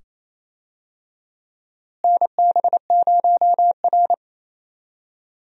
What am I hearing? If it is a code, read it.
Morse code, heard as a keyed tone, about 27 wpm, 700 Hz. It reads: NB0R
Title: USA Call Signs - 4 Characters - Repeated 2x - 28wpm